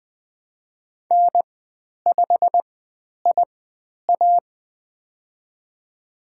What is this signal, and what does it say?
Morse code, heard as a keyed tone, about 20 wpm, spaced as for 10 wpm, 700 Hz. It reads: N5IA
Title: 102 Lesson 17g - Letter u - Call Signs 20wpm